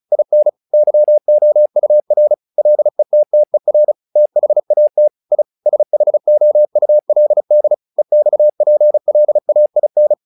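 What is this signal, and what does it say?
Morse code, heard as a keyed tone, about 36 wpm, 600 Hz. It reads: IN YOUR LETTER THAT I SHOULD EXPLAIN
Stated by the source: Morse practice text "Tower Trouble" sent at 35 WPM, from the SKCC - Straight Key Century Club learning center audio files.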